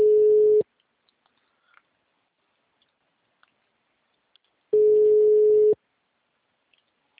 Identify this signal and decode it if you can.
Ringback tone (Europe, 425 Hz).